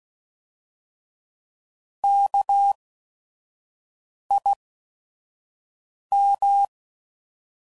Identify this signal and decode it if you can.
Morse code, heard as a keyed tone, about 16 wpm, spaced as for 5 wpm, 800 Hz. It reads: KIM
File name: Prefixes / By First Letter / Ks